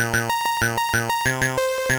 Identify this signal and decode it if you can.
There is no code.